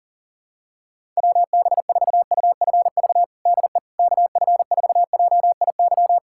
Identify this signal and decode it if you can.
Morse code, heard as a keyed tone, about 40 wpm, 700 Hz. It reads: WB4UFV DE KF4JIY
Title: USA Callsign DE Callsign - 6 Characters - 40wpm